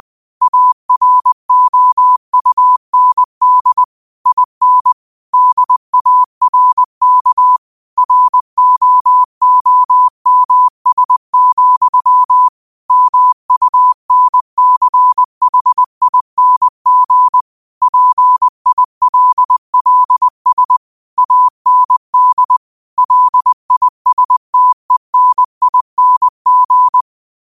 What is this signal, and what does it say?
Morse code, heard as a keyed tone, about 20 wpm, 1000 Hz. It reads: AROUND IN DARK ROOMS, MUNCHING PILLS AND LISTENING
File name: qotdfiles / 20wpm